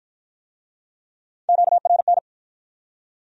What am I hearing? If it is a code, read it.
Morse code, heard as a keyed tone, about 53 wpm, 700 Hz. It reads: QRN